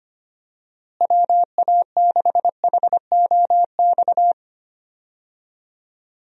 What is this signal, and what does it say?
Morse code, heard as a keyed tone, about 25 wpm, 700 Hz. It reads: WA6HOX
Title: USA Call Signs - 6 Characters - Repeated 2x - 25wpm